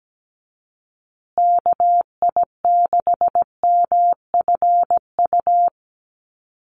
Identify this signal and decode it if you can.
Morse code, heard as a keyed tone, about 17 wpm, 700 Hz. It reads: KI6MFU
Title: USA Call Signs Repeated 3x 17wpm